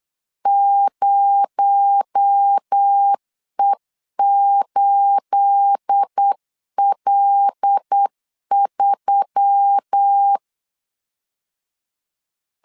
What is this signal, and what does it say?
Morse code, heard as a keyed tone, about 8 wpm, 800 Hz. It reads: 0E8L3
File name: LEZIONI / 9 28 8wpm 3512